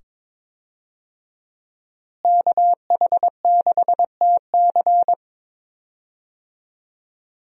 Morse code, heard as a keyed tone, about 22 wpm, 700 Hz. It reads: KH6TC
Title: DXCC-100 Call Signs - 5 Characters - Repeated 2x - 22wpm